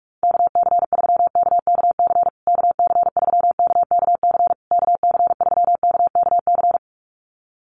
Morse code, heard as a keyed tone, about 48 wpm, 690 Hz. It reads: KC3KKC KC3KKC KC3KKC
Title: USA Call Signs Repeated 3x 45wpm